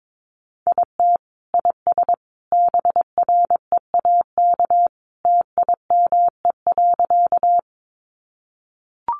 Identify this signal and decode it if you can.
Morse code, heard as a keyed tone, about 22 wpm, 700 Hz. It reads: IT IS BREAK TIME.
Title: Easy Phrases 22wpm